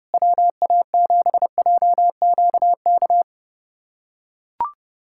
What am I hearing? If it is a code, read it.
Morse code, heard as a keyed tone, about 32 wpm, 700 Hz. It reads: WA7JQK
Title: USA Callsign DE Callsign 30wpm